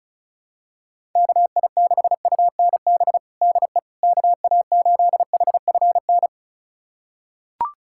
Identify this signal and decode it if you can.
Morse code, heard as a keyed tone, about 36 wpm, 700 Hz. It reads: KI6UNB DE KA8HFN
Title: USA Callsign DE Callsign - 6 Characters - 35wpm